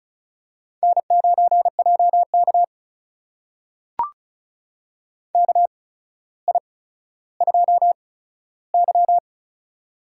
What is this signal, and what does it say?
Morse code, heard as a keyed tone, about 36 wpm, spaced as for 15 wpm, 700 Hz. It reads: N9JK KI2Y
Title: USA Call Signs - 4 Characters - 8x Character Spacing - 35wpm